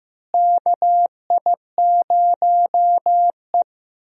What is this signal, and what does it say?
Morse code, heard as a keyed tone, about 15 wpm, 700 Hz. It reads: KI0E
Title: USA Call Signs - 4 Characters - Repeated 3x - 15wpm